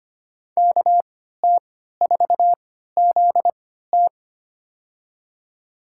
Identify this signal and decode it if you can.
Morse code, heard as a keyed tone, about 25 wpm, spaced as for 14 wpm, 700 Hz. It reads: KT4ZT
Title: USA Call Signs - 5 Characters - 3x Character Spacing - 25wpm